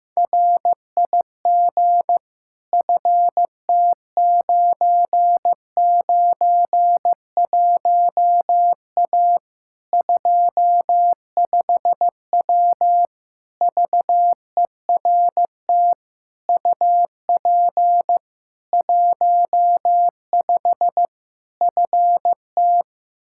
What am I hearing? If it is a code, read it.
Morse code, heard as a keyed tone, about 15 wpm, 700 Hz. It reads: RIG FT991A 25W VERT UP 15 FT
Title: Practice QSO - Exchange 2 - Info - 15wpm